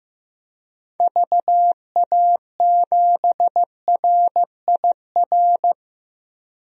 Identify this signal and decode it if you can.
Morse code, heard as a keyed tone, about 15 wpm, 700 Hz. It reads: VA7RIR
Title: DXCC-100 Call Signs - 6 Characters - 15wpm